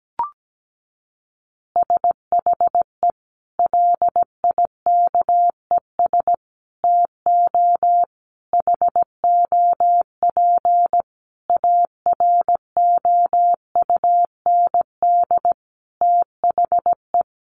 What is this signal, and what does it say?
Morse code, heard as a keyed tone, about 17 wpm, 700 Hz. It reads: SHE LIKES TO HOP AROUND THE
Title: Easy Phrases 17wpm